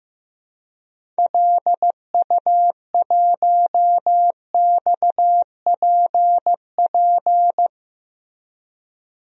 Morse code, heard as a keyed tone, about 16 wpm, 700 Hz. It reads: LU1XPP
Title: DXCC-100 Call Signs - 6 Characters - 15wpm